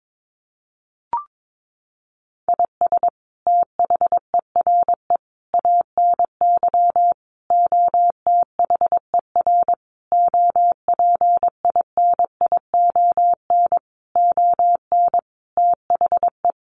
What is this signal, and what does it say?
Morse code, heard as a keyed tone, about 22 wpm, 700 Hz. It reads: IS THERE ANY OTHER OPINION ON THE
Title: Sentences from Top 1200 Words 22wpm